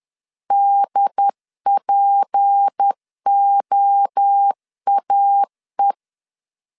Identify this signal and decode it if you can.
Morse code, heard as a keyed tone, about 10 wpm, 800 Hz. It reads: DPOAE
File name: LEZIONI / 7 GF9Z / 7 G 10wpm 2501